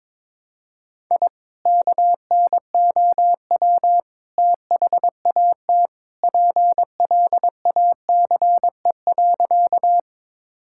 Morse code, heard as a keyed tone, about 22 wpm, 700 Hz. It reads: I KNOW THAT PLACE.
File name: practice / 22wpm / Sentences with 2-Word Phrases - 22wpm